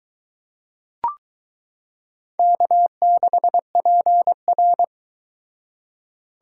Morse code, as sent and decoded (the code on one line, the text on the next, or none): -.- -.... .--. .-.
K6PR